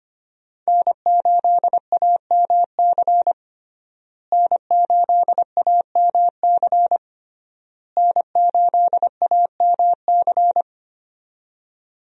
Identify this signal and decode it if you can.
Morse code, heard as a keyed tone, about 25 wpm, 700 Hz. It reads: N8AMC N8AMC N8AMC
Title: USA Call Signs - 5 Characters - Repeated 3x - 25wpm